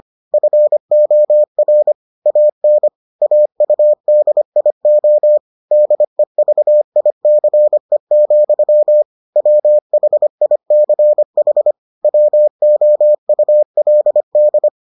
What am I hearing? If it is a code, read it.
Morse code, heard as a keyed tone, about 25 wpm, 600 Hz. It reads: FOR AN AUDIO DEVICE, WHICH WOULD